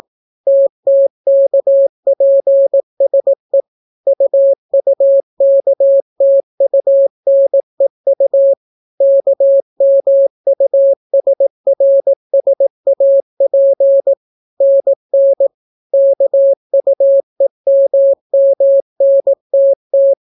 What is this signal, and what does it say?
Morse code, heard as a keyed tone, about 18 wpm, 550 Hz. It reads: TTKPSE UUKTUNEU KMUSRSAP NN KUEMMNTT